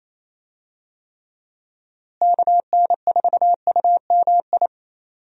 Morse code, heard as a keyed tone, about 28 wpm, 700 Hz. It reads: KN4UMI
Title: USA Call Signs - 6 Characters - Repeated 3x - 28wpm